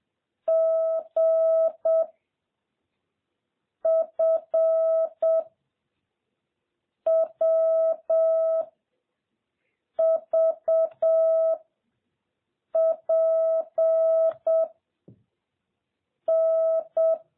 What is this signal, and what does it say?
Morse code, heard as a keyed tone, about 7 wpm, 650 Hz. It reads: G F W V P N
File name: jh8cbh / 2021cw